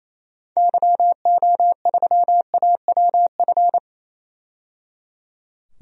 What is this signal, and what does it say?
Morse code, heard as a keyed tone, about 28 wpm, 700 Hz. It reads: YO3AWF
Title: DXCC-100 Call Signs - 6 Characters - 28wpm